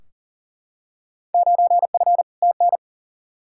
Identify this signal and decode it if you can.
Morse code, heard as a keyed tone, about 37 wpm, 700 Hz. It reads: 9F TN
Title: Field Day Exchanges Encore - 40wpm